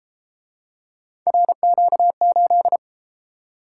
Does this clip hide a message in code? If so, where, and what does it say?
Yes: Morse code from 1.3 to 2.8 s, heard as a keyed tone, about 31 wpm, 700 Hz. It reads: RQ8